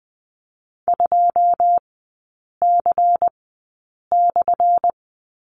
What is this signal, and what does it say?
Morse code, heard as a keyed tone, about 20 wpm, spaced as for 8 wpm, 700 Hz. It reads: 2C/
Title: 178 Lesson 29e - Slash - Three Characters 20wpm